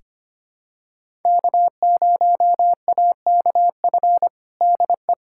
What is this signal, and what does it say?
Morse code, heard as a keyed tone, about 25 wpm, 700 Hz. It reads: K0AKF DE
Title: USA Callsign DE Callsign - 5 Characters - 25wpm